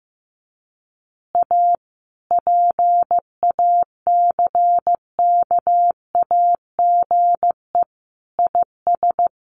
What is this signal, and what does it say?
Morse code, heard as a keyed tone, about 15 wpm, 700 Hz. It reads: A PACKAGE IS
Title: Easy Phrases 15wpm